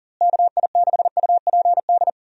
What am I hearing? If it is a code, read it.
Morse code, heard as a keyed tone, about 37 wpm, 700 Hz. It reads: KI6UPD
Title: USA Call Signs - 6 Characters - Repeated 3x - 40wpm